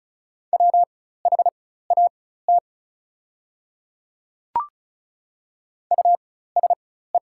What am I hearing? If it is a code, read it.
Morse code, heard as a keyed tone, about 34 wpm, spaced as for 16 wpm, 700 Hz. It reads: WHAT USE
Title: 4x Character Spacing - Top 100 Words - 35wpm